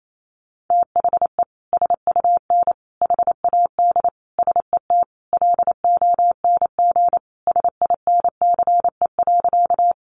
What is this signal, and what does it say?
Morse code, heard as a keyed tone, about 28 wpm, 700 Hz. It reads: THE SUN HAD SET LONG SINCE.
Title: Sentences from Top 300 Words 28wpm